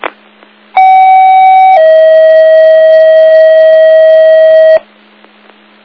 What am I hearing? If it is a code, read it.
Two-tone page, heard as a one-second tone, then a three-second tone, about 760 then 640 Hz. It is heard from 0.8 to 4.8 s.